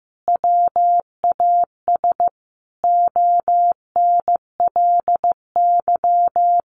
Morse code, heard as a keyed tone, about 15 wpm, 700 Hz. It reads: WAS ONLY
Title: Sentences from Top 1500 Words 15wpm